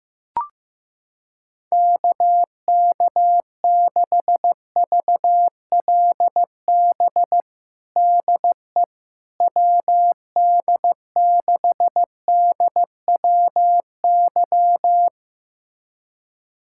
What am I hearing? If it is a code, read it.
Morse code, heard as a keyed tone, about 15 wpm, 700 Hz. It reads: KK6VLB DE WD6DWY
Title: USA Callsign DE Callsign - 6 Characters - 15wpm